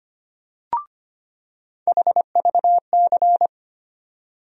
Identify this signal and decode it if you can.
Morse code, heard as a keyed tone, about 25 wpm, 700 Hz. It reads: HVC
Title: USA Call Sign Postfixes - 3 Letters 25wpm